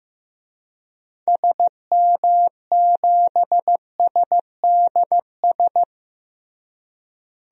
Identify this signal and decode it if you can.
Morse code, heard as a keyed tone, about 15 wpm, 700 Hz. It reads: SM7SDS
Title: DXCC-100 Call Signs - 6 Characters - Repeated 3x - 15wpm